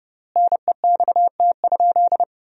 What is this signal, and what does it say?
Morse code, heard as a keyed tone, about 30 wpm, 700 Hz. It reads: NEXT?
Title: 4x Word-Spacing - Sentences from Top 300 Words - 30wpm